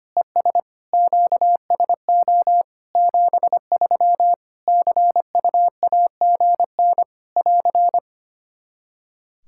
Morse code, heard as a keyed tone, about 25 wpm, 700 Hz. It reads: ES QSO 73 CUAGN <AR>
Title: K1USN SST QSO Practice 25wpm